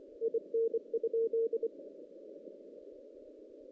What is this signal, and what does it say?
Morse code, heard as a keyed tone, about 24 wpm, 460 Hz. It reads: IN?
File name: 2016 JAN CW NAQP